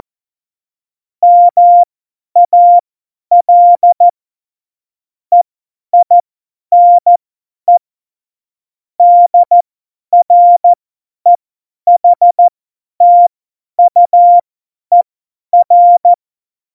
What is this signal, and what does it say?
Morse code, heard as a keyed tone, about 14 wpm, spaced as for 10 wpm, 700 Hz. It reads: MAL EINE DREHTUER
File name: Morsekurs / Witze / Witz 31. Oktober 17, Tempo 50